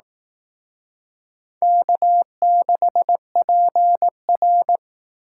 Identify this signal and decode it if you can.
Morse code, heard as a keyed tone, about 19 wpm, 700 Hz. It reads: K6PR